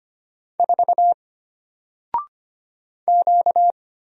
Morse code, heard as a keyed tone, about 25 wpm, 700 Hz. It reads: 4 Q
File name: Single-Letter-Number-25wpm